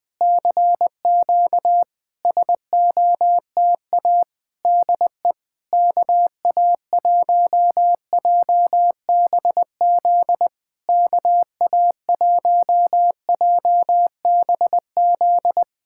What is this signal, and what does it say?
Morse code, heard as a keyed tone, about 20 wpm, 690 Hz. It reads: CQ SOTA DE KA1JBZ KA1JBZ
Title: SOTA QSO Practice 20wpm